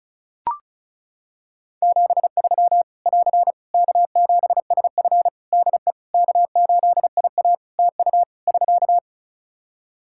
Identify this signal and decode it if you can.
Morse code, heard as a keyed tone, about 36 wpm, 700 Hz. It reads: 73 <AR> K7SF DE K8IA TU <SK>
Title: Practice QSO - Exchange 4 - Ending - 35wpm